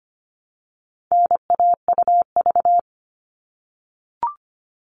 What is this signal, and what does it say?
Morse code, heard as a keyed tone, about 25 wpm, 700 Hz. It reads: NAUV
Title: All Cut Numbers - 4-Digit 25wpm